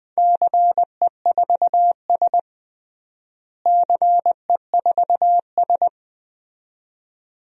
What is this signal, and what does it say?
Morse code, heard as a keyed tone, about 21 wpm, 700 Hz. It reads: CE4S CE4S
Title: DXCC-100 Call Signs - 4 Characters - Repeated 2x - 20wpm